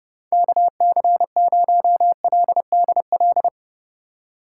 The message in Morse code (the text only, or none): KC0LDL